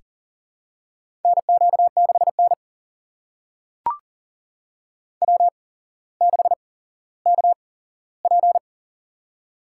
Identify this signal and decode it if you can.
Morse code, heard as a keyed tone, about 40 wpm, spaced as for 17 wpm, 700 Hz. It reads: NQ6N W6KP
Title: USA Call Signs - 4 Characters - 8x Character Spacing - 40wpm